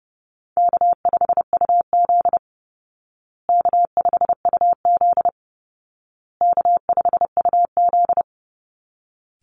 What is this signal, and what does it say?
Morse code, heard as a keyed tone, about 30 wpm, 700 Hz. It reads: K5UZ K5UZ K5UZ
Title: Easy Call Signs Repeated 3x - 30wpm